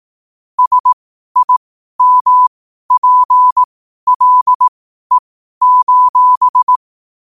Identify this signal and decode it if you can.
Morse code, heard as a keyed tone, about 18 wpm, spaced as for 12 wpm, 1000 Hz. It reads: SIMPLE:
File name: qotdfiles / 12wpm